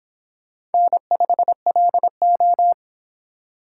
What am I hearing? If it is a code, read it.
Morse code, heard as a keyed tone, about 30 wpm, 700 Hz. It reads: N5LO